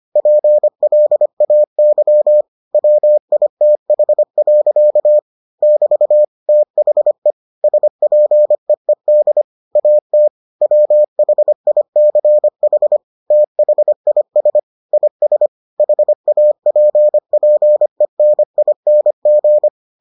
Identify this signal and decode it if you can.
Morse code, heard as a keyed tone, about 25 wpm, 600 Hz. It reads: PLAY WITH. <BT> THE SPEED AT WHICH THIS IS HAPPENING